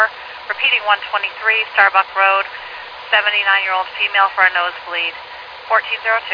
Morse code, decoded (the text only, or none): none